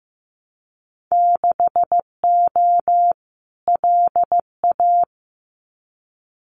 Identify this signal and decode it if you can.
Morse code, heard as a keyed tone, about 15 wpm, 700 Hz. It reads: 6O LA
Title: Winter Field Day Exchanges Encore 15wpm